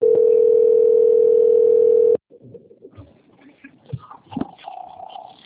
Ringback tone (North America), 440 + 480 Hz.
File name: out- - - - - .11373